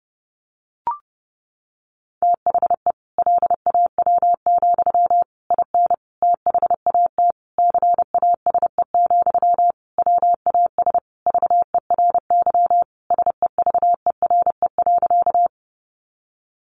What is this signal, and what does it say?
Morse code, heard as a keyed tone, about 30 wpm, 700 Hz. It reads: THE LAW, IN THAT CASE, WAS VERY SEVERE.
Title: Sentences from Top 1400 Words 30wpm